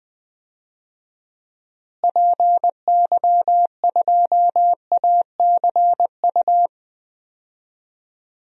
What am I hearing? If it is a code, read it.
Morse code, heard as a keyed tone, about 20 wpm, 700 Hz. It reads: PY2ACU
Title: DXCC-100 Call Signs - 6 Characters - Repeated 3x - 20wpm